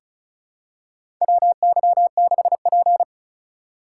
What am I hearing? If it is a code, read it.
Morse code, heard as a keyed tone, about 36 wpm, 700 Hz. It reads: WY6P
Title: USA Call Signs - 4 Characters - 3x Character Spacing - 35wpm